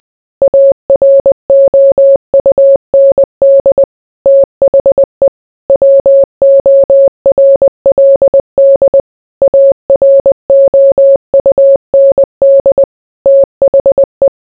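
Morse code, heard as a keyed tone, about 20 wpm, 550 Hz. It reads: AROUND THE WORLD AROUND THE
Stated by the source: Morse code mixed with noise and music as background sound for sleep.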